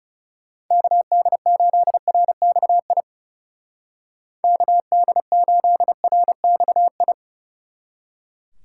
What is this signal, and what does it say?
Morse code, heard as a keyed tone, about 37 wpm, 700 Hz. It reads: KD8RXI KD8RXI
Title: Speed Racing - USA 2x3 Call Signs 30wpm